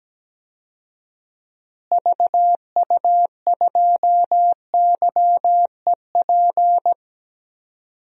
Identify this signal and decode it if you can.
Morse code, heard as a keyed tone, about 17 wpm, 700 Hz. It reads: VU2YEP
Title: DXCC-100 Call Signs - 6 Characters - Repeated 2x - 17wpm